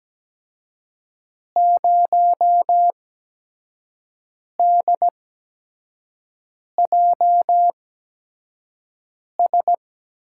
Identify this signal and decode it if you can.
Morse code, heard as a keyed tone, about 17 wpm, spaced as for 5 wpm, 700 Hz. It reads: 0DJS
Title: USA Call Signs - 8x Character Spacing - 17wpm